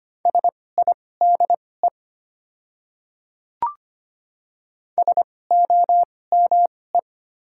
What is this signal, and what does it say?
Morse code, heard as a keyed tone, about 25 wpm, spaced as for 18 wpm, 700 Hz. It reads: SIDE SOME
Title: 2x Character Spacing - Top 100 Words - 25wpm